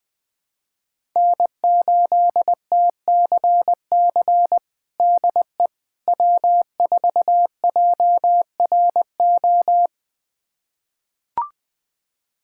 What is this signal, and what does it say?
Morse code, heard as a keyed tone, about 21 wpm, 700 Hz. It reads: N8TCC DE W4JRO
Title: USA Callsign DE Callsign - 5 Characters - 20wpm